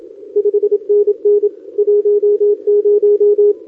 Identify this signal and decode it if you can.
Morse code, heard as a keyed tone, about 27 wpm, 410 Hz. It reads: 5NN 10
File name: OK QRP 2026 / OKQRP2026